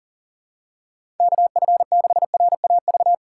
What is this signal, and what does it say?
Morse code, heard as a keyed tone, about 40 wpm, 700 Hz. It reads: KF6RAV
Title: USA Call Signs - 6 Characters - 3x Character Spacing - 40wpm